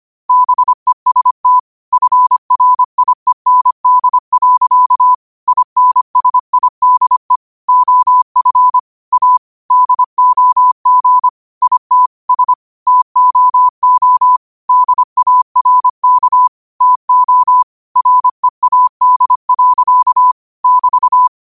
Morse code, heard as a keyed tone, about 25 wpm, 1000 Hz. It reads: BEST FRIEND. INSIDE OF A DOG IT S TOO DARK TO READ. <BT>
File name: qotdfiles / 25wpm